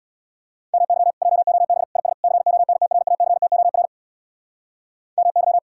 Morse code, heard as a keyed tone, about 80 wpm, 700 Hz. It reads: MY JOB IS CONTRACTOR MY